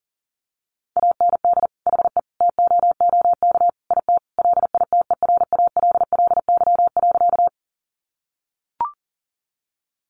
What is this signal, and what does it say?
Morse code, heard as a keyed tone, about 40 wpm, 700 Hz. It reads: AND HE TOOK IT LITERALLY.
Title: Sentences from Top 1400 Words 40wpm